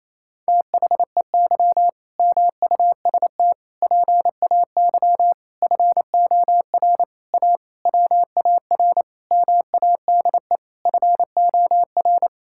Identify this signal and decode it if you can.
Morse code, heard as a keyed tone, about 28 wpm, 700 Hz. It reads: THEY MUST PAY FOR A WAR MADE FOR
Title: Sentences from Top 1000 Words - Review Part 6 - 28wpm